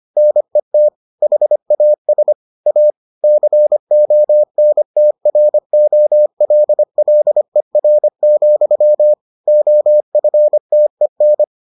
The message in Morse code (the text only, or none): NET HAS A CONTROLLER, OFTEN